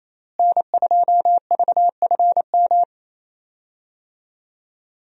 Morse code, heard as a keyed tone, about 28 wpm, 700 Hz. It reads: N2VFM
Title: USA Call Signs - 5 Characters - Repeated 3x - 28wpm